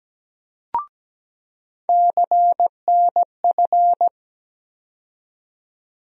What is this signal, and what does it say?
Morse code, heard as a keyed tone, about 17 wpm, 700 Hz. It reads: CNF
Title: USA Call Sign Postfixes - 3 Letters 17wpm